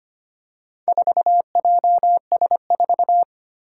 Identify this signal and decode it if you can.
Morse code, heard as a keyed tone, about 25 wpm, 700 Hz. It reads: 4JS4
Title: Four Challenging Letter-Number Combinations 25wpm